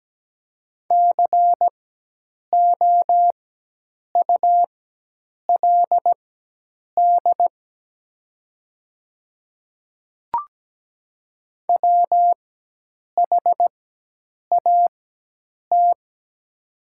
Morse code, heard as a keyed tone, about 17 wpm, spaced as for 8 wpm, 700 Hz. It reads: COULD WHAT
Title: 4x Character Spacing - Top 100 Words - 17wpm